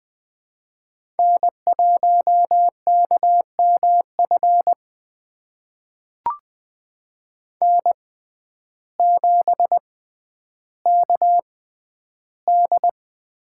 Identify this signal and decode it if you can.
Morse code, heard as a keyed tone, about 21 wpm, spaced as for 11 wpm, 700 Hz. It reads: N1KMF N7KD
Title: USA Call Signs - 5 Characters - 6x Character Spacing - 20wpm